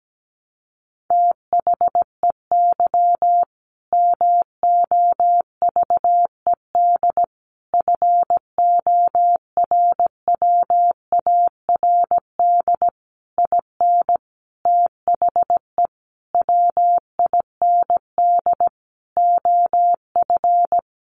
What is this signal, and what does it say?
Morse code, heard as a keyed tone, about 17 wpm, 700 Hz. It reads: THEY MOVED FORWARD IN THE WIND OF
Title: A Sound Of Thunder 17wpm